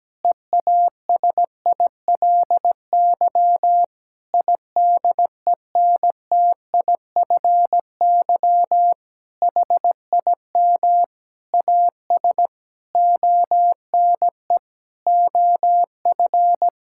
Morse code, heard as a keyed tone, about 17 wpm, 700 Hz. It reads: EASILY IDENTIFY HIM AS ONE OF
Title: Sentences from Top 1000 Words - Review Part 6 - 17wpm